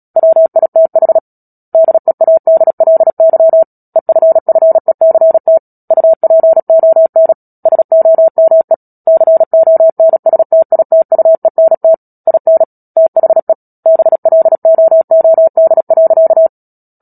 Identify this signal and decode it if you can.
Morse code, heard as a keyed tone, about 37 wpm, 650 Hz. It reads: WITH DEADLY EFFECT UPON SOME CONSTITUENT IN THE BLOOD.
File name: Book Two / 10-Chapter 10 - Epilogue - 36 WPM